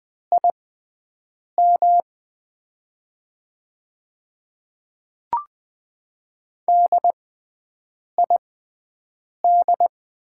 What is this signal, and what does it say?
Morse code, heard as a keyed tone, about 20 wpm, spaced as for 7 wpm, 700 Hz. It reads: IM DID
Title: 6x Character Spacing - Top 100 Words - 20wpm